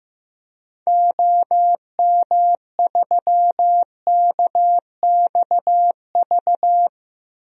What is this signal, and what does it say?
Morse code, heard as a keyed tone, about 15 wpm, 700 Hz. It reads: OM3KXV
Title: DXCC-100 Call Signs - 6 Characters - 15wpm